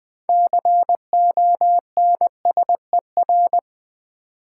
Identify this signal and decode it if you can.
Morse code, heard as a keyed tone, about 20 wpm, 700 Hz. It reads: CONSER
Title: Build A Word - 3 Letters At A Time - 8-Letter Words - 20wpm